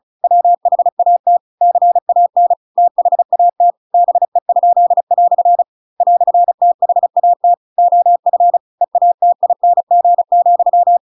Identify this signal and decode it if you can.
Morse code, heard as a keyed tone, about 34 wpm, 700 Hz. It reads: WHAT CAN THAT BE?" "THAT OF EATING,